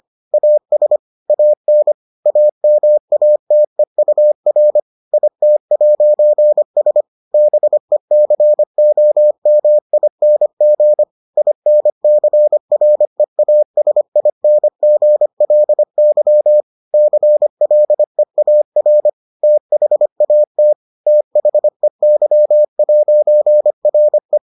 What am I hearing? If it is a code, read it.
Morse code, heard as a keyed tone, about 25 wpm, 600 Hz. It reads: AS AN AMATEUR IT'S BECOMING INCREASINGLY CLEAR THAT THEY'RE